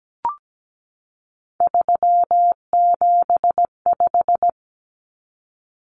Morse code, heard as a keyed tone, about 17 wpm, 700 Hz. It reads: 375
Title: Common Cut Numbers - 3-Digit 17wpm